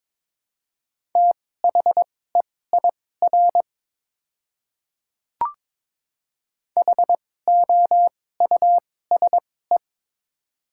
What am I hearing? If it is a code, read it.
Morse code, heard as a keyed tone, about 22 wpm, spaced as for 16 wpm, 700 Hz. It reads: THEIR HOUSE
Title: 2x Character Spacing - Top 300 Words - 22wpm